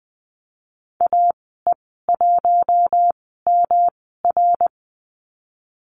Morse code, heard as a keyed tone, about 20 wpm, spaced as for 15 wpm, 700 Hz. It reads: AE1MR
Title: 122 Lesson 20g - Number 3 - Call Signs 20wpm